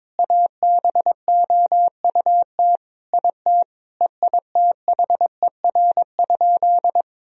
Morse code, heard as a keyed tone, about 22 wpm, 700 Hz. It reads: ABOUT IT EITHER?
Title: Sentences from Top 1000 Words - Review Part 4 - 22wpm